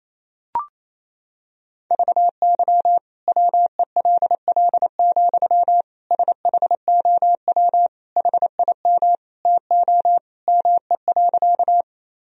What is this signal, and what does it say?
Morse code, heard as a keyed tone, about 28 wpm, 700 Hz. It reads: VY WELL, SHOW HIM TO ME.
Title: Sentences from Top 200 Words and Abbreviations 28wpm